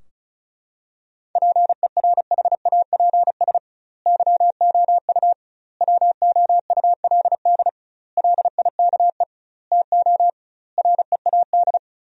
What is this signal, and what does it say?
Morse code, heard as a keyed tone, about 36 wpm, 700 Hz. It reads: PERHAPS YOU WOULD LIKE TO READ
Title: 2x Word-Spacing - Sentences from Top 500 Words - 35wpm